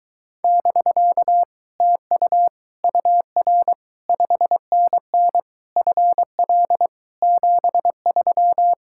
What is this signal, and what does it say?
Morse code, heard as a keyed tone, about 24 wpm, 700 Hz. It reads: <BK> TU UR 5NN FL 73